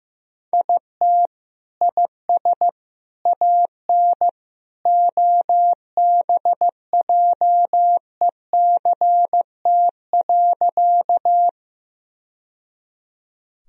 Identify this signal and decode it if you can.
Morse code, heard as a keyed tone, about 15 wpm, 700 Hz. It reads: IT IS AN OBJECT.
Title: Sentences from Top 1000 Words - Review Part 3 - 15wpm